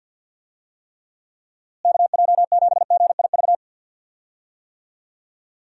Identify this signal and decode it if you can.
Morse code, heard as a keyed tone, about 47 wpm, 700 Hz. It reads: KJ7GIV
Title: USA Call Signs - 6 Characters - Repeated 2x - 50wpm